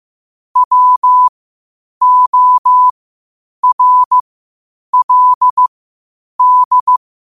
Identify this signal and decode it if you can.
Morse code, heard as a keyed tone, about 15 wpm, spaced as for 8 wpm, 1000 Hz. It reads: WORLD
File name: qotdfiles / 8wpm